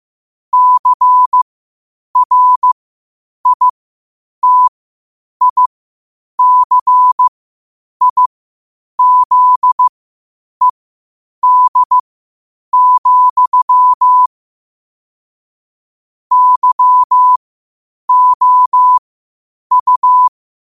Morse code, heard as a keyed tone, about 15 wpm, spaced as for 8 wpm, 1000 Hz. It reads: CRITICIZED, YOU
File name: qotdfiles / 8wpm